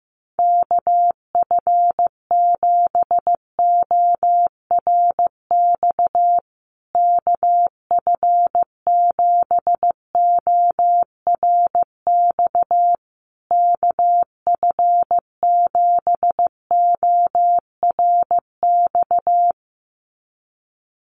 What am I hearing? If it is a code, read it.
Morse code, heard as a keyed tone, about 15 wpm, 700 Hz. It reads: KF7ORX KF7ORX KF7ORX
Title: USA Call Signs Repeated 3x 15wpm